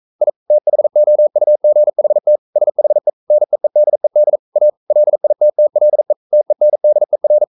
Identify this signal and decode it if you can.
Morse code, heard as a keyed tone, about 42 wpm, 600 Hz. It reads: I THOUGHT SHE NEEDED A LITTLE TENDER
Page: Romanen Never af Ken Follet på engelsk.